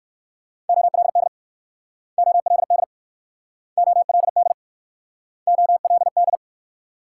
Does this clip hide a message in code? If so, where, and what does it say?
Yes: Morse code from 0.7 to 6.4 s, heard as a keyed tone, about 60 wpm, 700 Hz. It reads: OLD OLD OLD OLD